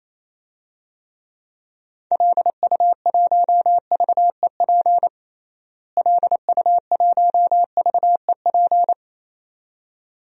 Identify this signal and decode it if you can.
Morse code, heard as a keyed tone, about 28 wpm, 700 Hz. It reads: LU1VEP LU1VEP
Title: DXCC-100 Call Signs - 6 Characters - Repeated 2x - 28wpm